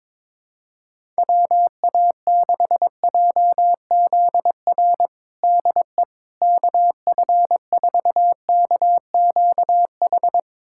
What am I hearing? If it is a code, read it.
Morse code, heard as a keyed tone, about 22 wpm, 700 Hz. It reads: WA6JZR DE KF4KQH
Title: USA Callsign DE Callsign - 6 Characters - 22wpm